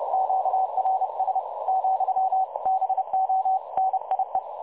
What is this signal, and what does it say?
Morse code, heard as a keyed tone, about 31 wpm, 760 Hz, with radio noise. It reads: QRS , BO NIE
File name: dlazaawansowanych / nagraniazpasma / websdr recording 3565.0kHz